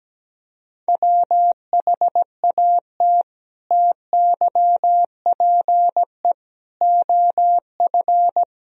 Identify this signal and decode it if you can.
Morse code, heard as a keyed tone, about 17 wpm, 700 Hz. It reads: WHAT TYPE OF
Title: Sentences from Top 1000 Words - Review Part 3 - 17wpm